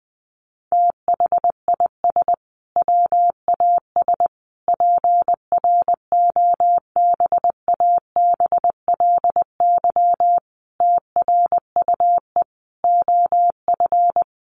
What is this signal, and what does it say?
Morse code, heard as a keyed tone, about 20 wpm, 700 Hz. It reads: THIS WAS PROBABLY TRUE OF
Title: Sentences from Top 700 Words 20wpm